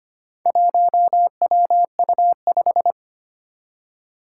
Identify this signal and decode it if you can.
Morse code, heard as a keyed tone, about 25 wpm, 700 Hz. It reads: 1WU5
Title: Four Challenging Letter-Number Combinations 25wpm